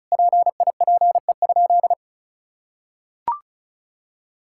Morse code, heard as a keyed tone, about 34 wpm, 700 Hz. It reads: PIPE?
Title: Sentences from Top 1600 Words 35wpm